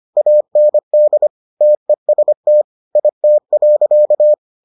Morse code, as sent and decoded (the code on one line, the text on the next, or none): .- -. -.. / - . ... - / .. - .-.-.-
AND TEST IT.